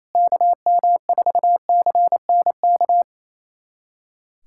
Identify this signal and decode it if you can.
Morse code, heard as a keyed tone, about 27 wpm, 700 Hz. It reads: KM4CNK
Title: USA Callsign DE Callsign - 6 Characters - 28wpm